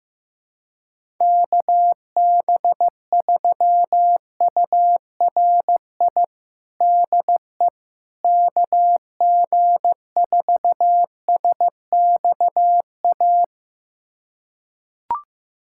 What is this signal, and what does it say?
Morse code, heard as a keyed tone, about 16 wpm, 700 Hz. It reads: KB3URI DE KG4SXA
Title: USA Callsign DE Callsign 15wpm